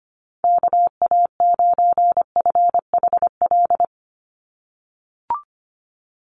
Morse code, heard as a keyed tone, about 25 wpm, 700 Hz. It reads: KA9FHL
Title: USA Call Signs 25wpm